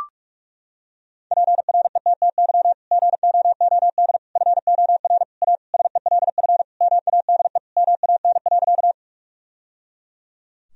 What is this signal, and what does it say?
Morse code, heard as a keyed tone, about 41 wpm, 700 Hz. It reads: PRETTY GOOD FOR A SELF MADE MAN.
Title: Sentences from Top 1000 Words - Review Part 1 - 45wpm